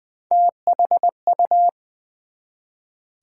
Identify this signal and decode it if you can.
Morse code, heard as a keyed tone, about 20 wpm, 700 Hz. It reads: THU
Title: Build A Word - 1 Letter At A Time - 5-Letter Words - 20wpm